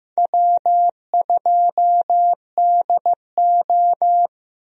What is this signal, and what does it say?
Morse code, heard as a keyed tone, about 15 wpm, 700 Hz. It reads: W2DO
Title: USA Call Signs - 4 Characters - Repeated 3x - 15wpm